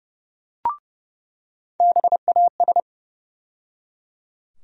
Morse code, heard as a keyed tone, about 32 wpm, 700 Hz. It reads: BAS
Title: USA Call Sign Postfixes - 3 Letters 30wpm